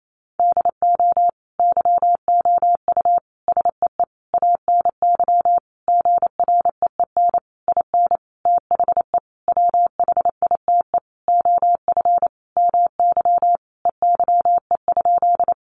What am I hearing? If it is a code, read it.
Morse code, heard as a keyed tone, about 28 wpm, 700 Hz. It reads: DO YOU SEE ANY GREEN IN THE WHITE OF MY EYE?
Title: Sentences from Top 500 Words 28wpm